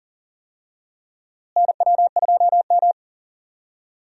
Morse code, heard as a keyed tone, about 40 wpm, 700 Hz. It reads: NW2M
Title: USA Call Signs - 4 Characters - Repeated 2x - 40wpm